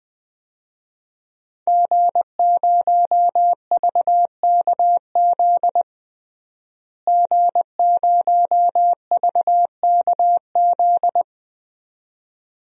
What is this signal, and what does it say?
Morse code, heard as a keyed tone, about 20 wpm, 700 Hz. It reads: G0VKZ G0VKZ
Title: DXCC-100 Call Signs - 5 Characters - Repeated 2x - 20wpm